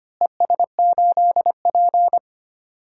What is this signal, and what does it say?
Morse code, heard as a keyed tone, about 25 wpm, 700 Hz. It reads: ES8P
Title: DXCC-100 Call Signs - 4 Characters - Repeated 2x - 25wpm